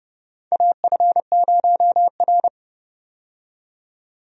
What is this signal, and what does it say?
Morse code, heard as a keyed tone, about 31 wpm, 700 Hz. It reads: AF0R
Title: USA Call Signs - 4 Characters - 30wpm